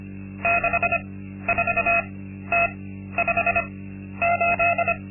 Morse code, heard as a keyed tone, about 25 wpm, spaced as for 12 wpm, 690 Hz. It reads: 64T58
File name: Linksmakalnis-2